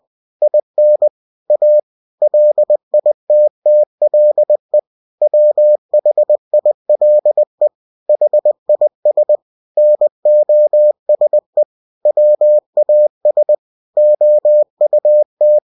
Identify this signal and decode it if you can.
Morse code, heard as a keyed tone, about 20 wpm, 600 Hz. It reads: IN A LITTLE WHILE HIS NOSE WAS OUT
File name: cw-books / winnie-the-pooh / winnie-20wpm-page-0016